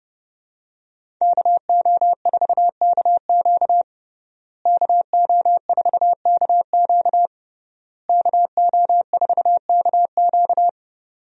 Morse code, heard as a keyed tone, about 30 wpm, 700 Hz. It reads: KO4KQ KO4KQ KO4KQ
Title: USA Call Signs - 5 Characters - Repeated 3x - 30wpm